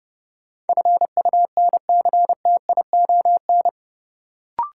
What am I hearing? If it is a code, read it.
Morse code, heard as a keyed tone, about 30 wpm, 700 Hz. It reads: FUNCTION
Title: Sentences from Top 1100 Words - New Word Drill - 30wpm